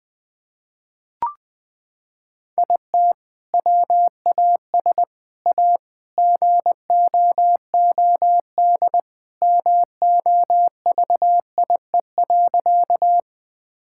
Morse code, heard as a keyed tone, about 21 wpm, 700 Hz. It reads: IT WAS A GOOD MOVIE.
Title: Sentences from Top 1000 Words - Review Part 5 - 20wpm